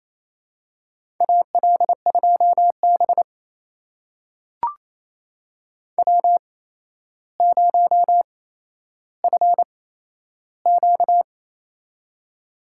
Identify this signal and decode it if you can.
Morse code, heard as a keyed tone, about 28 wpm, spaced as for 12 wpm, 700 Hz. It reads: AL2B W0FQ
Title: USA Call Signs - 8x Character Spacing - 28wpm